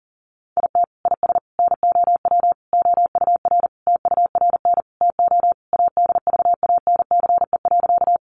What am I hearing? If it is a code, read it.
Morse code, heard as a keyed tone, about 40 wpm, 700 Hz. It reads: IT IS NOW OUR TURN TO ADVANCE.
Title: Sentences from Top 1400 Words 40wpm